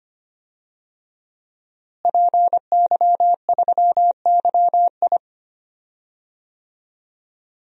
Morse code, heard as a keyed tone, about 25 wpm, 700 Hz. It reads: PY3YI
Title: DXCC-100 Call Signs - 5 Characters - Repeated 3x - 25wpm